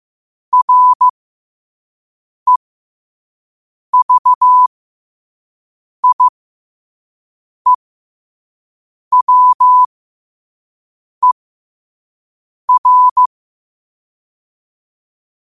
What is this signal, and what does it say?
Morse code, heard as a keyed tone, about 15 wpm, spaced as for 5 wpm, 1000 Hz. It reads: REVIEWER
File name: qotdfiles / 5wpm